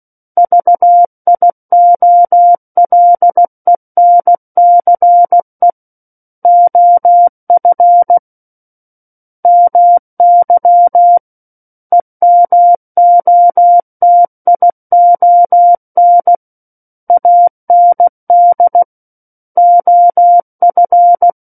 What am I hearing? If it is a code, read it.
Morse code, heard as a keyed tone, about 16 wpm, 700 Hz. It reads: VIOLENCE OF MY EMOTION AND OF
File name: Book One / 07-Chapter 7 - 16 WPM